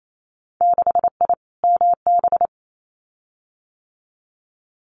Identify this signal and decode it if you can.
Morse code, heard as a keyed tone, about 28 wpm, 700 Hz. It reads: 6I MB
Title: Winter Field Day Exchanges Encore 28wpm